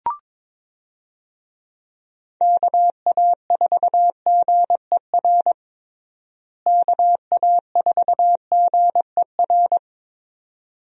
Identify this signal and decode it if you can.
Morse code, heard as a keyed tone, about 22 wpm, 700 Hz. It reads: KA4GER KA4GER
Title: USA Call Signs - 6 Characters - Repeated 2x - 22wpm